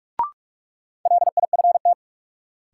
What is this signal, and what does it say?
Morse code, heard as a keyed tone, about 42 wpm, 700 Hz. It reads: LIFT
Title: Sentences from Top 1000 Words - New Word Drill - 45wpm